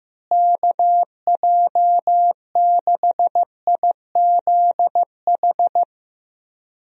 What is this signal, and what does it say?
Morse code, heard as a keyed tone, about 15 wpm, 700 Hz. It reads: KJ6IZH
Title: USA Call Signs - 6 Characters - Repeated 2x - 15wpm